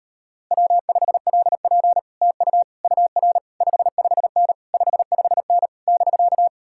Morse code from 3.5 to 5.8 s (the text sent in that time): 55N 55N